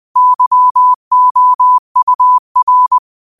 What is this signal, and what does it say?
Morse code, heard as a keyed tone, about 20 wpm, 1000 Hz. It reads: YOUR
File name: qotdfiles / 20wpm